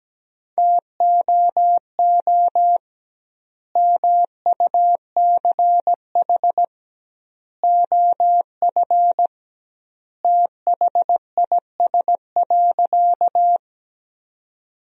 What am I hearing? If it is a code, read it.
Morse code, heard as a keyed tone, about 17 wpm, 700 Hz. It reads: TOO MUCH OF THIS.
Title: 2x Word-Spacing - Sentences from Top 200 Words - 17wpm